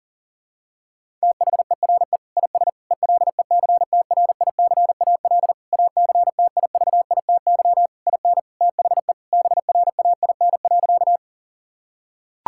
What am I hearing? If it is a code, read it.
Morse code, heard as a keyed tone, about 40 wpm, 700 Hz. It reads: THERE IS ELECTRICAL ACTIVITY IN THE BRAIN.
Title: Sentences from Top 1000 Words - Review Part 6 - 40wpm